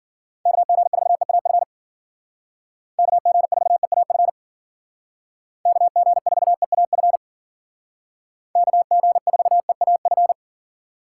Morse code, heard as a keyed tone, about 49 wpm, 700 Hz. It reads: KG4EAF KG4EAF KG4EAF KG4EAF
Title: Speed Racing - USA Call Signs 40wpm